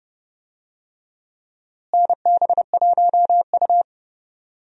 Morse code, heard as a keyed tone, about 30 wpm, 700 Hz. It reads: NB1U
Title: USA Call Signs - 4 Characters - Repeated 2x - 30wpm